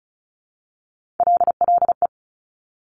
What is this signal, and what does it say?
Morse code, heard as a keyed tone, about 36 wpm, 700 Hz. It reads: LLE
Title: Top Three Letter Combinations in Words 35wpm